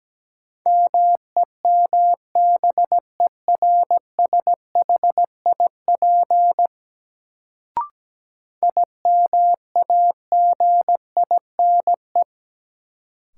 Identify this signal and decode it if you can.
Morse code, heard as a keyed tone, about 17 wpm, 700 Hz. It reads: MEMBERSHIP IMAGINE
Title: Sentences from Top 1400 Words - New Word Drill - 17wpm